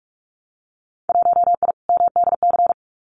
Morse code, heard as a keyed tone, about 46 wpm, 700 Hz. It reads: 1I MDC
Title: Winter Field Day Exchanges 45wpm